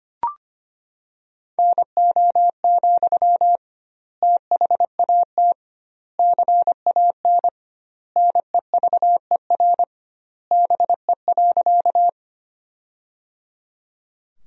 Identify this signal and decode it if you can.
Morse code, heard as a keyed tone, about 25 wpm, 700 Hz. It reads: NO, THAT CAN NEVER BE.
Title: 2x Word-Spacing - Sentences from Top 300 Words - 25wpm